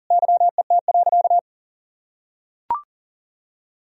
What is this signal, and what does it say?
Morse code, heard as a keyed tone, about 40 wpm, 700 Hz. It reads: YET.
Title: Sentences from Top 1000 Words - Review Part 1 - 40wpm